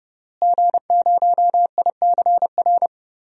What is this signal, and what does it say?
Morse code, heard as a keyed tone, about 30 wpm, 700 Hz. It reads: G0ICR
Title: DXCC-100 Call Signs - 5 Characters - Repeated 3x - 30wpm